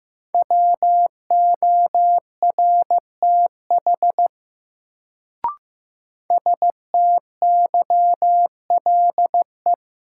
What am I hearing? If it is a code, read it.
Morse code, heard as a keyed tone, about 15 wpm, 700 Hz. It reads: WORTH STYLE
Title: Sentences from Top 800 Words - New Word Drill - 15wpm